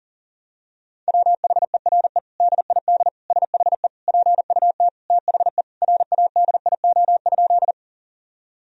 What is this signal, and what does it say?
Morse code, heard as a keyed tone, about 40 wpm, 700 Hz. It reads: WHERE DID SHE PUT THE RADIO?
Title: Sentences from Top 1000 Words - Review Part 3 - 40wpm